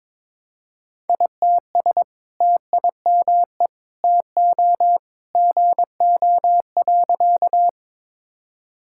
Morse code, heard as a keyed tone, about 22 wpm, 700 Hz. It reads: ITS TIME TO GO.
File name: practice / 22wpm / Sentences with 2-Word Phrases - 22wpm